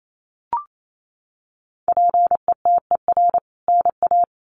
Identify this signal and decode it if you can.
Morse code, heard as a keyed tone, about 28 wpm, 700 Hz. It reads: PETER NA
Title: CWT Exchange 28wpm